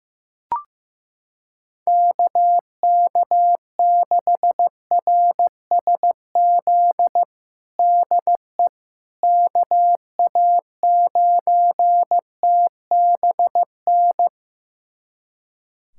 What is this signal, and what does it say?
Morse code, heard as a keyed tone, about 16 wpm, 700 Hz. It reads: KK6RSZ DE KA9TBN
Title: USA Callsign DE Callsign 15wpm